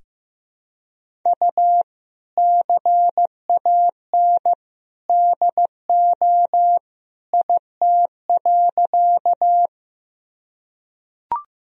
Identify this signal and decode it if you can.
Morse code, heard as a keyed tone, about 16 wpm, 700 Hz. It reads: U CAN DO IT.
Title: Sentences from Top 100 Words and Abbreviations 15wpm